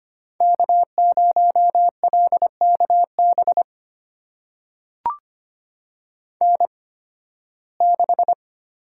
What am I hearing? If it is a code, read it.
Morse code, heard as a keyed tone, about 25 wpm, spaced as for 16 wpm, 700 Hz. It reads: K0LKB N6
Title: USA Call Signs - 5 Characters - 8x Character Spacing - 25wpm